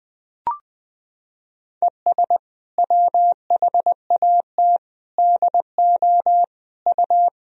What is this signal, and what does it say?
Morse code, heard as a keyed tone, about 20 wpm, 700 Hz. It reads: ES WHAT DO U
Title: Sentences from Top 100 Words and Abbreviations 20wpm